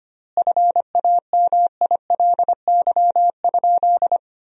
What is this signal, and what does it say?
Morse code, heard as a keyed tone, about 25 wpm, 700 Hz. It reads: FAMILY?
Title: 4x Word-Spacing - Sentences from Top 400 Words - 25wpm